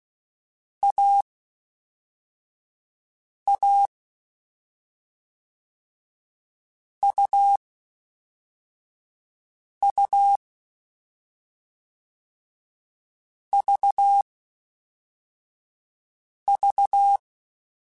Morse code, heard as a keyed tone, about 16 wpm, 800 Hz. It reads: A A U U V V
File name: Difficult Combos / Difficult Combos-mp3s / AUV AUV4 @5